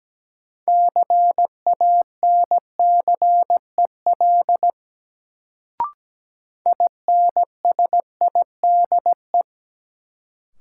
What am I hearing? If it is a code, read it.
Morse code, heard as a keyed tone, about 17 wpm, 700 Hz. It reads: CANCEL INSIDE
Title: Sentences from Top 1200 Words - New Word Drill Alt - 17wpm